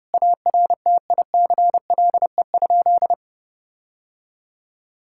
Morse code, heard as a keyed tone, about 30 wpm, 700 Hz. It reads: ARTICLE?
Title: Sentences from Top 1000 Words - Review Part 5 - 30wpm